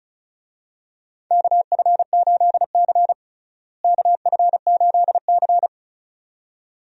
Morse code, heard as a keyed tone, about 36 wpm, 700 Hz. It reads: KF8C KF8C
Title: USA Call Signs - 4 Characters - Repeated 2x - 35wpm